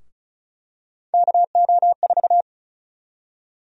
Morse code, heard as a keyed tone, about 34 wpm, 700 Hz. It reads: KO4
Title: USA Call Sign Prefixes - Repeated 2x - 35wpm